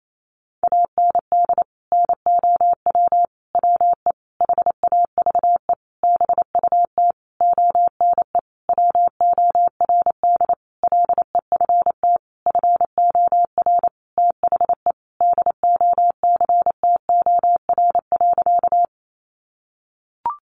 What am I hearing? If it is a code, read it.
Morse code, heard as a keyed tone, about 28 wpm, 700 Hz. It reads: AND NOW WE HAVE BUT ONE WORD LEFT FOR THE DOCTOR.
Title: Sentences from Top 1500 Words 28wpm